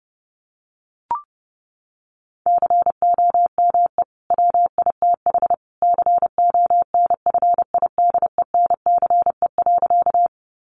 Morse code, heard as a keyed tone, about 30 wpm, 700 Hz. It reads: COME WITH CONFIDENCE.
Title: Sentences from Top 1500 Words 30wpm